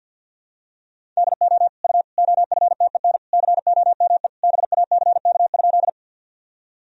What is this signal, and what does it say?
Morse code, heard as a keyed tone, about 46 wpm, 700 Hz. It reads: DO U OFTEN COME BACK?
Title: Sentences from Top 300 Words and Abbreviations 50wpm